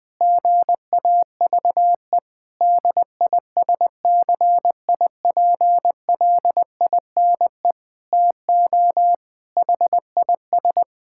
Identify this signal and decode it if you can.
Morse code, heard as a keyed tone, about 21 wpm, 700 Hz. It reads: GAVE DISCIPLINE TO HIS
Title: Sentences from Top 1000 Words - Review Part 6 - 20wpm